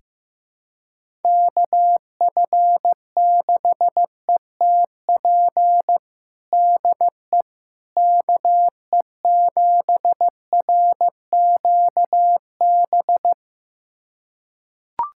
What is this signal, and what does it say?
Morse code, heard as a keyed tone, about 15 wpm, 700 Hz. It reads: KF6ETP DE KE7RQB
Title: USA Callsign DE Callsign - 6 Characters - 15wpm